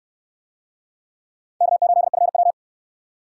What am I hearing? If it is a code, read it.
Morse code, heard as a keyed tone, about 62 wpm, 700 Hz. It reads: K8FW